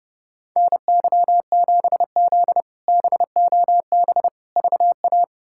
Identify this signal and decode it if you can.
Morse code, heard as a keyed tone, about 30 wpm, 700 Hz. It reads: NY7Z BOB VA
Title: K1USN SST QSO Practice - 30wpm